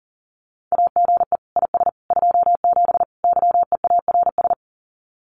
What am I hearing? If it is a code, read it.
Morse code, heard as a keyed tone, about 40 wpm, 700 Hz. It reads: AGE IS 27 YEARS
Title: Common QSO Phrases Encore 40wpm